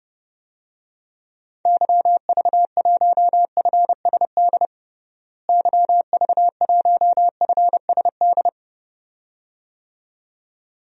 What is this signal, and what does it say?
Morse code, heard as a keyed tone, about 30 wpm, 700 Hz. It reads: YV1FSD YV1FSD
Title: DXCC-100 Call Signs - 6 Characters - Repeated 2x - 30wpm